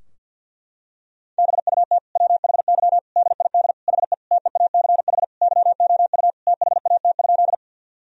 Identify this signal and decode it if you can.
Morse code, heard as a keyed tone, about 52 wpm, 700 Hz. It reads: BUT WHY DID HE TEACH YOU THAT?
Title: Sentences from Top 1000 Words - Review Part 2 - 50wpm